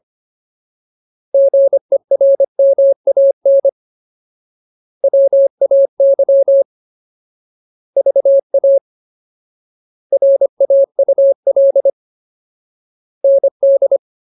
Morse code, heard as a keyed tone, about 25 wpm, 550 Hz. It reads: GERMAN WAY VA RAUL ND
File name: Ultimate Word List / 25WPM Ultimate